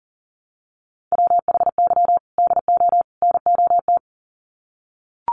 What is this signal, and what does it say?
Morse code, heard as a keyed tone, about 40 wpm, 700 Hz. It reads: WHY DO NOT
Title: Sets of 3 Words 40wpm